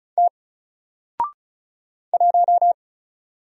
Morse code, heard as a keyed tone, about 36 wpm, 700 Hz. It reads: T 1
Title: Single Character (Letter:Number:Punctuation:Prosign) Alt 35wpm